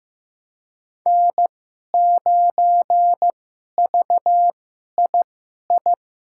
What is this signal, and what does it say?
Morse code, heard as a keyed tone, about 15 wpm, spaced as for 11 wpm, 700 Hz. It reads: N9VII
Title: USA Call Signs - 2x Character Spacing - 15wpm